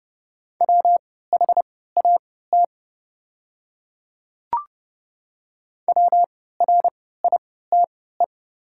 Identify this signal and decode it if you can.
Morse code, heard as a keyed tone, about 30 wpm, spaced as for 17 wpm, 700 Hz. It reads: WHAT WRITE
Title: 3x Character Spacing - Top 100 Words - 30wpm